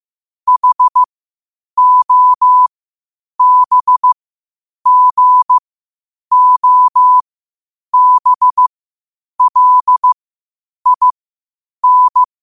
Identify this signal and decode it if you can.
Morse code, heard as a keyed tone, about 15 wpm, spaced as for 8 wpm, 1000 Hz. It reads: HOBGOBLIN
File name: qotdfiles / 8wpm